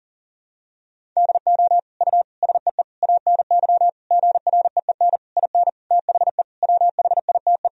Morse code, heard as a keyed tone, about 37 wpm, 700 Hz. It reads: DO U SEE ANY GREEN IN THE WHITE
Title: Sentences from Top 500 Words and Abbreviations 40wpm